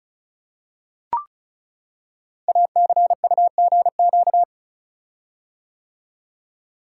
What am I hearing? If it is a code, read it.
Morse code, heard as a keyed tone, about 36 wpm, 700 Hz. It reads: ACUGQ
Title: Code Groups - 5 Letters - 35wpm